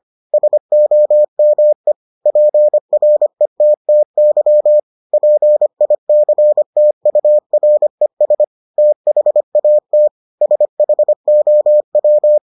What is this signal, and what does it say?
Morse code, heard as a keyed tone, about 25 wpm, 600 Hz. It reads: SOME PRETTY PICTURES THAT SHOW